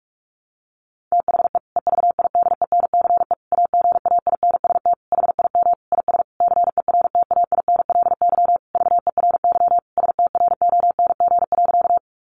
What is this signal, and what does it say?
Morse code, heard as a keyed tone, about 44 wpm, 700 Hz. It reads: THE EVIDENCE AGAINST HIM IS CERTAINLY VERY STRONG.
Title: Sentences from Top 1300 Words 45wpm